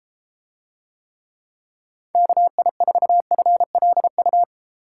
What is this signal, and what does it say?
Morse code, heard as a keyed tone, about 31 wpm, 700 Hz. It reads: KI4FLU